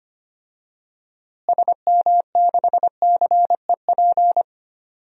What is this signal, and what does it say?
Morse code, heard as a keyed tone, about 25 wpm, 700 Hz. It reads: SM6CEP
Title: DXCC-100 Call Signs - 6 Characters - Repeated 3x - 25wpm